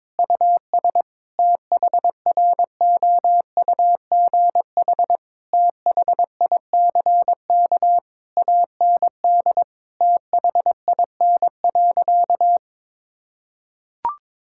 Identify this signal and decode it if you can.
Morse code, heard as a keyed tone, about 22 wpm, 700 Hz. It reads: US THROUGH THICK AND THIN.
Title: Sentences from Top 1600 Words 22wpm